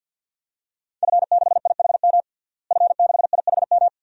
Morse code, heard as a keyed tone, about 46 wpm, 700 Hz. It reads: F6IHM F6IHM
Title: DXCC-100 Call Signs - 5 Characters - Repeated 3x - 50wpm